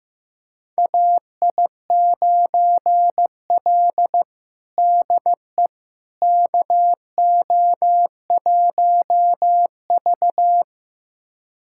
Morse code, heard as a keyed tone, about 15 wpm, 700 Hz. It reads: AI9L DE KO1V
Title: USA Callsign DE Callsign - 4 Characters - 15wpm